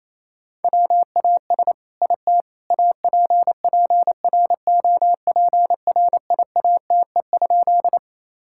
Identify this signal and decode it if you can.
Morse code, heard as a keyed tone, about 28 wpm, 700 Hz. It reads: WAS IT APPROPRIATE?
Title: Sentences from Top 1000 Words - Review Part 5 - 28wpm